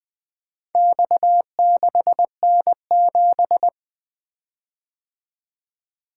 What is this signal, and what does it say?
Morse code, heard as a keyed tone, about 20 wpm, 700 Hz. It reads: X6N7
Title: Four Letter-Number Combinations - 20wpm